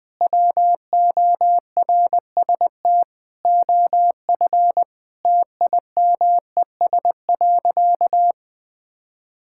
Morse code, heard as a keyed tone, about 21 wpm, 700 Hz. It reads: WORST OF TIMES.
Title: Sentences with 8-Word Phrases - Spoken Beginning - 20wpm